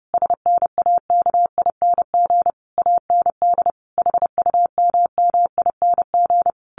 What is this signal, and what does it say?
Morse code, heard as a keyed tone, about 30 wpm, 700 Hz. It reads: SNAKING AND HUMMING
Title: A Sound Of Thunder 30wpm